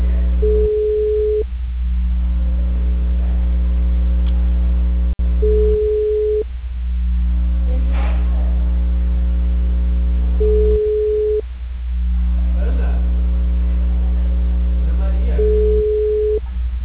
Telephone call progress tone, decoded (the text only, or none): ringback tone